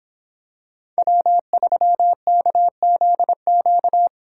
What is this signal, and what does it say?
Morse code, heard as a keyed tone, about 26 wpm, 700 Hz. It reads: W3KZQ